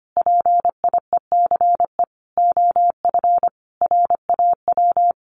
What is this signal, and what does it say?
Morse code, heard as a keyed tone, about 25 wpm, 700 Hz. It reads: PIECE OF RAW
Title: Sentences from Top 1500 Words 25wpm